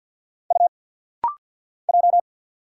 Morse code, heard as a keyed tone, about 50 wpm, 700 Hz. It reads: U J
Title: Single Letters - Rapid-Fire 50wpm